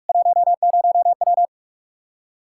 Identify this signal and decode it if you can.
Morse code, heard as a keyed tone, about 46 wpm, 700 Hz. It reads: 10W
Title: Practice QSO - Exchange 2 - Info - 45wpm